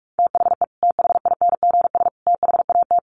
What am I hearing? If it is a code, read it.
Morse code, heard as a keyed tone, about 44 wpm, 700 Hz. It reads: THE THINGS THAT
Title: Sets of 3 Words 45wpm